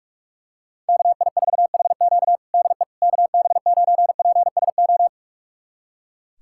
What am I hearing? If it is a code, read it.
Morse code, heard as a keyed tone, about 43 wpm, 700 Hz. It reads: KI4HQ DE KB9PSO
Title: USA Callsign DE Callsign 45wpm